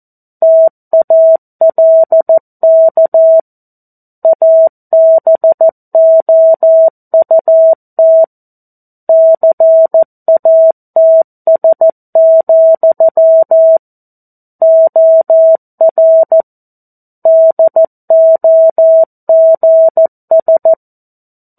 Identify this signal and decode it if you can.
Morse code, heard as a keyed tone, about 14 wpm, 650 Hz. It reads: TALK ABOUT CATS, OR DOGS
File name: 02-Chapter-02 14 WPM